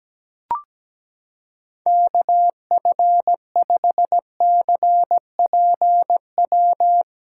Morse code, heard as a keyed tone, about 17 wpm, 700 Hz. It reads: KF5CPW
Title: USA Callsign DE Callsign - 6 Characters - 17wpm